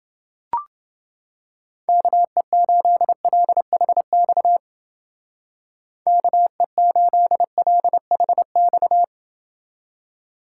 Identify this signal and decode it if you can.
Morse code, heard as a keyed tone, about 29 wpm, 700 Hz. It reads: KE8LHX KE8LHX